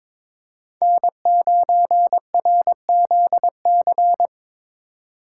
Morse code, heard as a keyed tone, about 22 wpm, 700 Hz. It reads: N9RZC
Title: USA Call Signs - 5 Characters - 3x Character Spacing - 22wpm